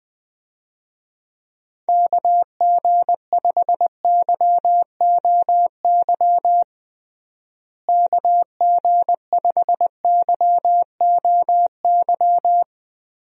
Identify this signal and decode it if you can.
Morse code, heard as a keyed tone, about 19 wpm, 700 Hz. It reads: KG5YOY KG5YOY
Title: USA Call Signs - 6 Characters - Repeated 3x - 20wpm